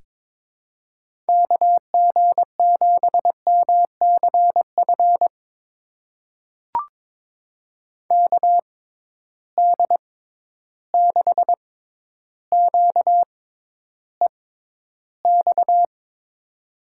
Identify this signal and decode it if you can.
Morse code, heard as a keyed tone, about 22 wpm, spaced as for 11 wpm, 700 Hz. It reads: KG7MCF KD6QEX
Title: USA Call Signs - 6x Character Spacing - 22wpm